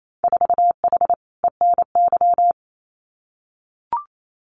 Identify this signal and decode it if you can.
Morse code, heard as a keyed tone, about 28 wpm, 700 Hz. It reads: 4H ENY
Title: Winter Field Day Exchanges Encore 28wpm